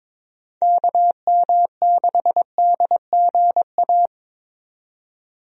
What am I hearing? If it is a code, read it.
Morse code, heard as a keyed tone, about 22 wpm, 700 Hz. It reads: KM6DGA
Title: USA Call Signs - 6 Characters - 22wpm